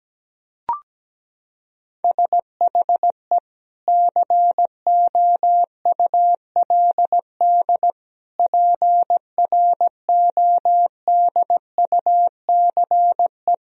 Morse code, heard as a keyed tone, about 17 wpm, 700 Hz. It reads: SHE COULD PRODUCE
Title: Sentences from Top 1000 Words - Review Part 3 - 17wpm